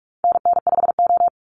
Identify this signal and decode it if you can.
Morse code, heard as a keyed tone, about 44 wpm, 700 Hz. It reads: NN5O
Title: Call Signs Repeated 3x - 45wpm